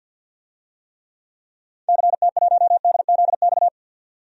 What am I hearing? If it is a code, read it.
Morse code, heard as a keyed tone, about 49 wpm, 700 Hz. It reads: CT1DZX
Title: DXCC-100 Call Signs - 6 Characters - 50wpm